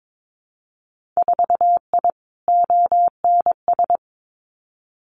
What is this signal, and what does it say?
Morse code, heard as a keyed tone, about 22 wpm, 700 Hz. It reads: 4I ONS
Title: Winter Field Day Exchanges Encore 22wpm